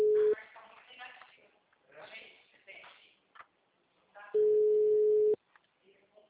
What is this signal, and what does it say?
Ringback tone (Europe, 425 Hz), with speech.